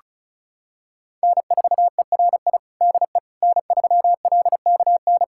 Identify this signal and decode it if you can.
Morse code, heard as a keyed tone, about 36 wpm, 700 Hz. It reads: N4ERI DE N3LKN
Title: USA Callsign DE Callsign - 5 Characters - 35wpm